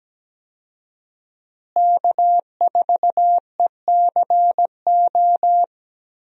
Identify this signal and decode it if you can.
Morse code, heard as a keyed tone, about 17 wpm, 700 Hz. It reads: K4ECO
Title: USA Call Signs - 5 Characters - Repeated 3x - 17wpm